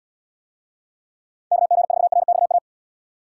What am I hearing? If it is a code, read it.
Morse code, heard as a keyed tone, about 77 wpm, 700 Hz. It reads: KG4RXA